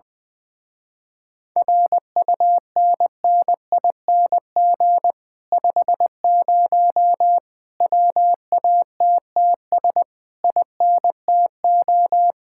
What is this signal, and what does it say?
Morse code, heard as a keyed tone, about 20 wpm, 700 Hz. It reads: RUNNING 50 WATTS INTO